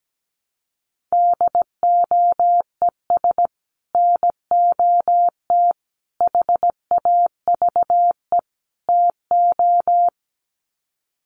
Morse code, heard as a keyed tone, about 17 wpm, 700 Hz. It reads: DOES NOT HAVE TO
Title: Sets of 4 Words 17wpm